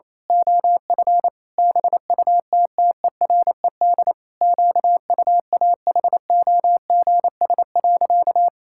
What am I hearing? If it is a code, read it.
Morse code, heard as a keyed tone, about 28 wpm, 700 Hz. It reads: OF BUTTERED QUAHOGS.
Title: Pangram Challenge v2 - 28wpm